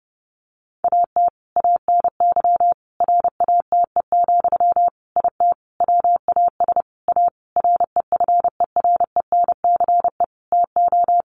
Morse code, heard as a keyed tone, about 30 wpm, 700 Hz. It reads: AT ANY RATE, IT WAS A REFERENCE TO
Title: Sentences from Top 1200 Words 30wpm